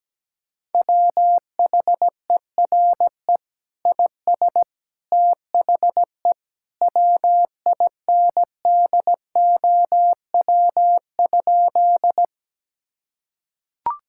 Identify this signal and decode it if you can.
Morse code, heard as a keyed tone, about 17 wpm, 700 Hz. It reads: WHERE IS THE WINDOW?
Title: Sentences from Top 1600 Words 17wpm